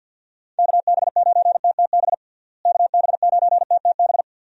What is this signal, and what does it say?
Morse code, heard as a keyed tone, about 53 wpm, 700 Hz. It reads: KB9TTB KB9TTB
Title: USA Call Signs - 6 Characters - Repeated 2x - 50wpm